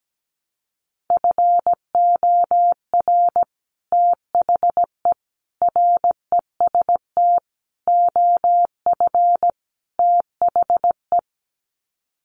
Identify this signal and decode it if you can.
Morse code, heard as a keyed tone, about 17 wpm, 700 Hz. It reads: FOR THE REST OF THE
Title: Sets of 5 Words 17wpm